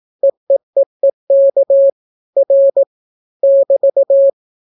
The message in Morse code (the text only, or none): EEEEK R <BT>